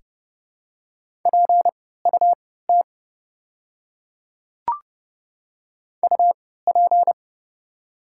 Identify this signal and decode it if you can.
Morse code, heard as a keyed tone, about 30 wpm, spaced as for 17 wpm, 700 Hz. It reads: PUT UP
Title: 3x Character Spacing - Top 200 Words - 30wpm